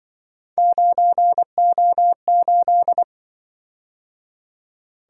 Morse code, heard as a keyed tone, about 24 wpm, 700 Hz. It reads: 9O8
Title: Speed-Racing - Three Letter-Number Combinations - 20wpm